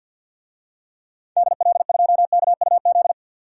Morse code, heard as a keyed tone, about 52 wpm, 700 Hz. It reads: DL2KUZ
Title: DXCC-100 Call Signs - 6 Characters - 50wpm